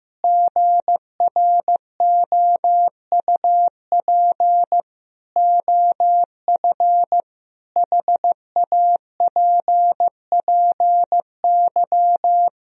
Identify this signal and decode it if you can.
Morse code, heard as a keyed tone, about 15 wpm, 700 Hz. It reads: GROUP OF HAPPY
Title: Sentences from Top 1000 Words - Review Part 1 - 15wpm